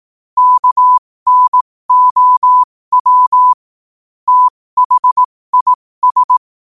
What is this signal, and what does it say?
Morse code, heard as a keyed tone, about 18 wpm, spaced as for 15 wpm, 1000 Hz. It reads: KNOW THIS
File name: qotdfiles / 15wpm